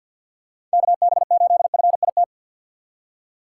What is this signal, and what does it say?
Morse code, heard as a keyed tone, about 52 wpm, 700 Hz. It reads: KB8FIT
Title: USA Call Signs - 6 Characters - 8x Character Spacing - 50wpm